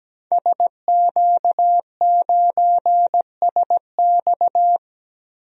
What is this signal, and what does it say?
Morse code, heard as a keyed tone, about 17 wpm, 700 Hz. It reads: SQ9SX
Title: DXCC-100 Call Signs - 5 Characters - Repeated 3x - 17wpm